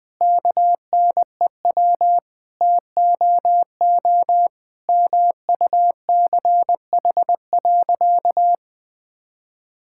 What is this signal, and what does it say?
Morse code, heard as a keyed tone, about 20 wpm, 700 Hz. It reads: KNEW TOO MUCH.
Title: Sentences from Top 1000 Words - Review Part 5 - 20wpm